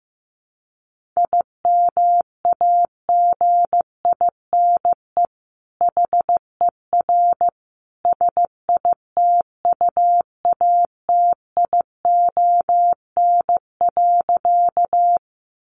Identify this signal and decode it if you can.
Morse code, heard as a keyed tone, about 15 wpm, 700 Hz. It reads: IMAGINE HER SITUATION.
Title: Sentences from Top 1400 Words 15wpm